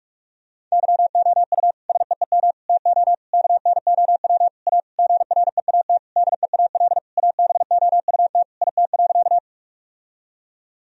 Morse code, heard as a keyed tone, about 42 wpm, 700 Hz. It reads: YOU SEEM TO KNOW A GREAT DEAL ABOUT IT.
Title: Sentences from Top 1000 Words - Review Part 1 - 45wpm